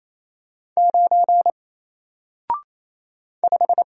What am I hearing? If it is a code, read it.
Morse code, heard as a keyed tone, about 28 wpm, 700 Hz. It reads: 9 5
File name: Single-Letter-Number-28wpm